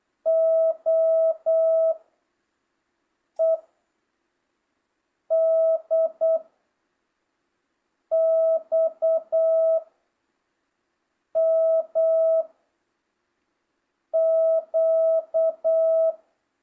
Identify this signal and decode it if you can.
Morse code, heard as a keyed tone, about 8 wpm, 650 Hz. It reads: O E D X M Q